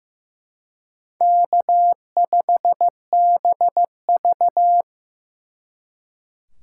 Morse code, heard as a keyed tone, about 18 wpm, 700 Hz. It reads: K5BV
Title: Speed Racing - USA Call Signs 15wpm